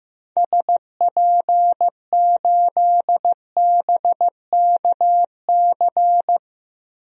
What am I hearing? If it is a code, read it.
Morse code, heard as a keyed tone, about 16 wpm, 700 Hz. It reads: SP8BKC
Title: DXCC-100 Call Signs - 6 Characters - Repeated 2x - 15wpm